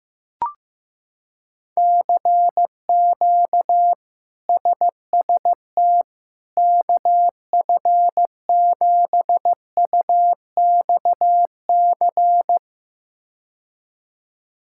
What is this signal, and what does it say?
Morse code, heard as a keyed tone, about 15 wpm, 700 Hz. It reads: CQ SST KF7UXC
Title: K1USN SST QSO Practice - 15wpm